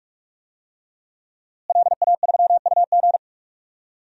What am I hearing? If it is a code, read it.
Morse code, heard as a keyed tone, about 46 wpm, 700 Hz. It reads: LA3UG
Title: DXCC-100 Call Signs - 5 Characters - Repeated 2x - 45wpm